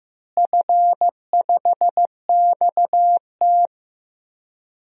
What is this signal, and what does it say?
Morse code, heard as a keyed tone, about 15 wpm, 700 Hz. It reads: F5XT
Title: DXCC-100 Call Signs - 4 Characters - Repeated 3x - 15wpm